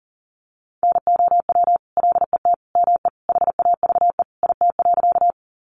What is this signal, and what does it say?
Morse code, heard as a keyed tone, about 40 wpm, 700 Hz. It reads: NOW LET ME HAVE IT.
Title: Sentences from Top 300 Words 40wpm